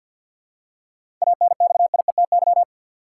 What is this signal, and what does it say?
Morse code, heard as a keyed tone, about 53 wpm, 700 Hz. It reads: ANXIETY
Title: Seven Letter Words - 50wpm